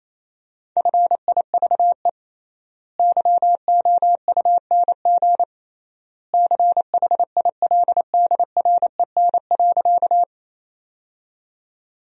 Morse code, heard as a keyed tone, about 28 wpm, 700 Hz. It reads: FIVE YOUNG CHILDREN.
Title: 3x Word-Spacing - Sentences from Top 400 Words - 28wpm